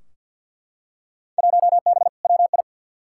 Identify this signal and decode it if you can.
Morse code, heard as a keyed tone, about 50 wpm, 700 Hz. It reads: 1B WI
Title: Field Day Exchanges Encore - 50wpm